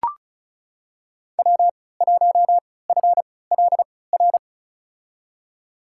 Morse code, heard as a keyed tone, about 36 wpm, spaced as for 20 wpm, 690 Hz. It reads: W1FLR
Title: USA Call Signs - 5 Characters - 3x Character Spacing - 35wpm